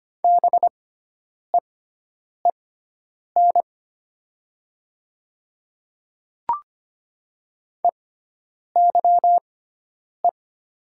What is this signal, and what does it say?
Morse code, heard as a keyed tone, about 25 wpm, spaced as for 9 wpm, 700 Hz. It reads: BEEN EYE
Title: 6x Character Spacing - Top 300 Words - 25wpm